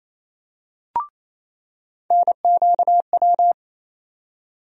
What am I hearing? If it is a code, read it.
Morse code, heard as a keyed tone, about 28 wpm, 700 Hz. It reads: NQW
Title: USA Call Sign Postfixes - 3 Letters 28wpm